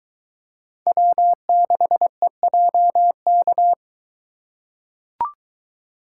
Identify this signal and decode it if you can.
Morse code, heard as a keyed tone, about 22 wpm, 700 Hz. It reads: W6EJK